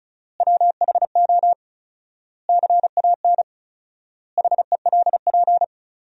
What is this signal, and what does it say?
Morse code, heard as a keyed tone, about 36 wpm, 700 Hz. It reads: WHO CAN HELP
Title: 4x Word-Spacing - Sentences from Top 200 Words - 35wpm